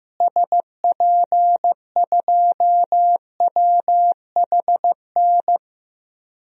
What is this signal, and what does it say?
Morse code, heard as a keyed tone, about 15 wpm, 700 Hz. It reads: SP2WHN
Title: DXCC-100 Call Signs - 6 Characters - Repeated 2x - 15wpm